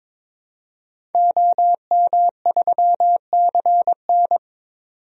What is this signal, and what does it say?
Morse code, heard as a keyed tone, about 21 wpm, 700 Hz. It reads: OM3CN
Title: DXCC-100 Call Signs - 5 Characters - 22wpm